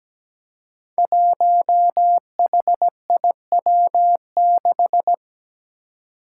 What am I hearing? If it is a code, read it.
Morse code, heard as a keyed tone, about 17 wpm, 700 Hz. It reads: 1HIW6
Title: Five Challenging Letter-Number Combinations 17wpm